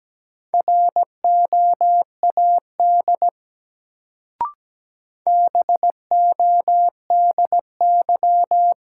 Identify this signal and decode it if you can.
Morse code, heard as a keyed tone, about 17 wpm, 700 Hz. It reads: ROAD BODY
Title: Sentences from Top 400 Words - New Word Drill - 17wpm